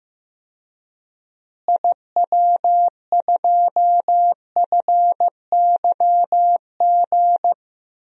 Morse code, heard as a keyed tone, about 15 wpm, 700 Hz. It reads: IW2FYG
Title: DXCC-100 Call Signs - 6 Characters - Repeated 3x - 15wpm